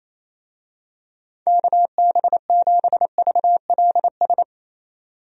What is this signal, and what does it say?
Morse code, heard as a keyed tone, about 27 wpm, 700 Hz. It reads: KB7VLS
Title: USA Call Signs - 6 Characters - 28wpm